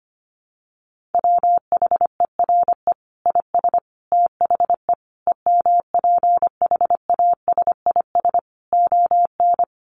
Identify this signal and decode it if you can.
Morse code, heard as a keyed tone, about 25 wpm, 700 Hz. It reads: WHERE IS THE EMPHASIS ON
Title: Sentences from Top 1400 Words 25wpm